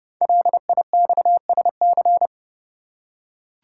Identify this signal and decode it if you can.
Morse code, heard as a keyed tone, about 31 wpm, 700 Hz. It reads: LIXSC
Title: Code Groups - 5 Letters - 30wpm